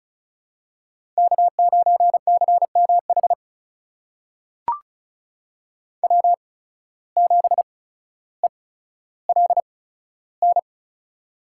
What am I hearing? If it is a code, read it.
Morse code, heard as a keyed tone, about 36 wpm, spaced as for 15 wpm, 700 Hz. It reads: K9CMH W7ELN
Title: USA Call Signs - 5 Characters - 8x Character Spacing - 35wpm